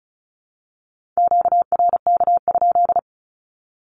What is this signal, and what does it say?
Morse code, heard as a keyed tone, about 34 wpm, 700 Hz. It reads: QRK?
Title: Q-Codes 35wpm